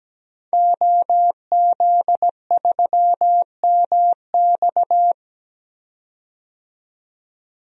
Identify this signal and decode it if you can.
Morse code, heard as a keyed tone, about 17 wpm, 700 Hz. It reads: OZ3MX
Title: DXCC-100 Call Signs - 5 Characters - 17wpm